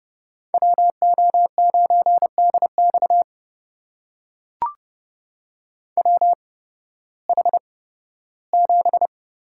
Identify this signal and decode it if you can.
Morse code, heard as a keyed tone, about 30 wpm, spaced as for 16 wpm, 700 Hz. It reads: WO9DX WH7
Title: USA Call Signs - 5 Characters - 8x Character Spacing - 30wpm